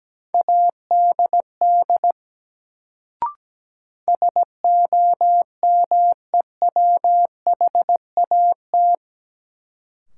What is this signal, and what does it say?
Morse code, heard as a keyed tone, about 17 wpm, 700 Hz. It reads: ADD SOMEWHAT
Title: Sentences from Top 1000 Words - Review Part 1 - New Word Drill - 17wpm